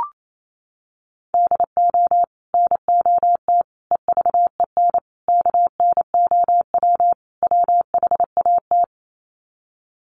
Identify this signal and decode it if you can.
Morse code, heard as a keyed tone, about 28 wpm, 700 Hz. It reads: DO NOT EVEN KNOW WHAT
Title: Sets of 5 Words Encore 28wpm